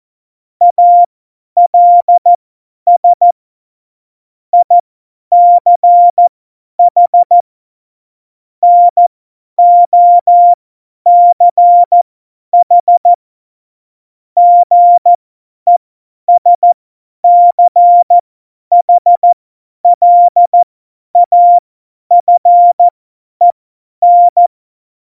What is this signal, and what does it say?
Morse code, heard as a keyed tone, about 14 wpm, spaced as for 10 wpm, 700 Hz. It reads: ALS ICH NOCH GESCHLAFEN
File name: Morsekurs / Witze / Witz 14. Mai 2019, 50 F70